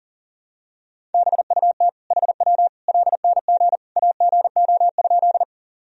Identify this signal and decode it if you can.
Morse code, heard as a keyed tone, about 40 wpm, 700 Hz. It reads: BUT HW LNG AGO?
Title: Sentences from Top 500 Words and Abbreviations 40wpm